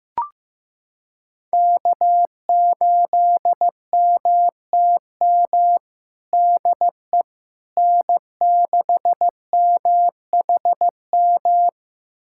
Morse code, heard as a keyed tone, about 15 wpm, 700 Hz. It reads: K8MTM DE N6MHM
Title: USA Callsign DE Callsign - 5 Characters - 15wpm